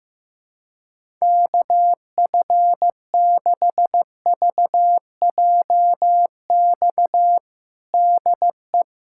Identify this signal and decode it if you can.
Morse code, heard as a keyed tone, about 15 wpm, 700 Hz. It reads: KF6VJX DE
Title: USA Callsign DE Callsign - 6 Characters - 15wpm